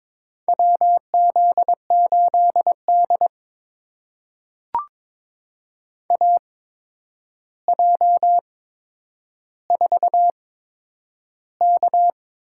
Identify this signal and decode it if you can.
Morse code, heard as a keyed tone, about 22 wpm, spaced as for 9 wpm, 700 Hz. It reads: WZ8D AJ4K
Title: USA Call Signs - 8x Character Spacing - 22wpm